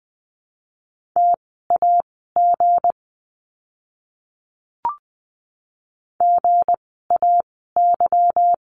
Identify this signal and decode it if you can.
Morse code, heard as a keyed tone, about 20 wpm, spaced as for 15 wpm, 700 Hz. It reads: TAG GAY
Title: 160 Lesson 26f - Letter g - Words 20wpm